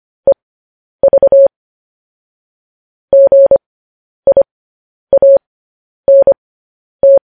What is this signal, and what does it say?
Morse code, heard as a keyed tone, about 25 wpm, spaced as for 10 wpm, 550 Hz. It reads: EV GIANT
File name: podcast / News headlines-1x-25x10